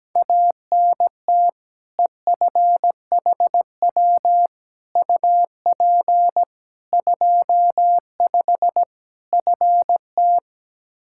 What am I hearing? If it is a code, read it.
Morse code, heard as a keyed tone, about 17 wpm, 700 Hz. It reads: ANT EFHW UP 25 FT
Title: K1USN SST QSO Practice 17wpm